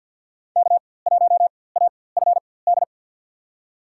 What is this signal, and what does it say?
Morse code, heard as a keyed tone, about 48 wpm, spaced as for 23 wpm, 700 Hz. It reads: K1AFD
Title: USA Call Signs - 4x Character Spacing - 50wpm